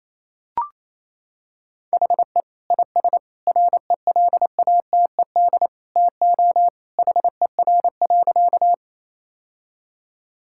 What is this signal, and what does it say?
Morse code, heard as a keyed tone, about 28 wpm, 700 Hz. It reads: HE IS RELATED TO HER.
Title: Sentences from Top 1000 Words - Review Part 5 - 28wpm